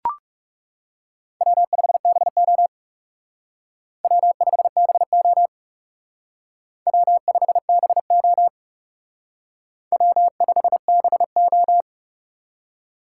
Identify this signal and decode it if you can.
Morse code, heard as a keyed tone, about 37 wpm, 700 Hz. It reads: W5BO W5BO W5BO W5BO
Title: Speed Racing - USA 1x2 Call Signs 30wpm